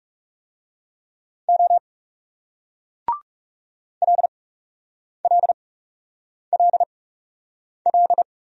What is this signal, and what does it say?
Morse code, heard as a keyed tone, about 38 wpm, 700 Hz. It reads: O L L L L